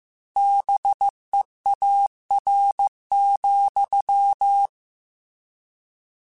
Morse code, heard as a keyed tone, about 15 wpm, 800 Hz. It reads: BEAR,